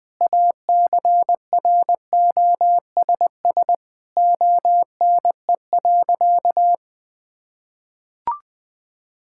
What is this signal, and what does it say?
Morse code, heard as a keyed tone, about 20 wpm, 700 Hz. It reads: ACROSS ONE.
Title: Sentences from Top 1000 Words - Review Part 5 - 20wpm